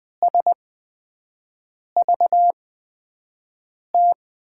Morse code, heard as a keyed tone, about 20 wpm, spaced as for 5 wpm, 700 Hz. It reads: SVT
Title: USA Call Signs - 8x Character Spacing - 20wpm